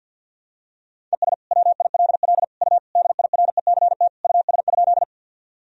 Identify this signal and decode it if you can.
Morse code, heard as a keyed tone, about 47 wpm, 700 Hz. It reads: ES WILL U DIRECT US?
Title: Sentences from Top 400 Words and Abbreviations 50wpm